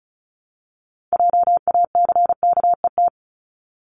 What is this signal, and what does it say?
Morse code, heard as a keyed tone, about 36 wpm, 690 Hz. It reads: JACKET
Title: Multi-letter Phonemes 35wpm